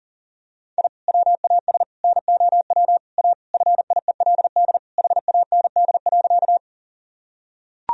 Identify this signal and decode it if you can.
Morse code, heard as a keyed tone, about 37 wpm, 700 Hz. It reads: I WAS NOW A FIELD HAND.
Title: Sentences from Top 1000 Words - Review Part 2 - 40wpm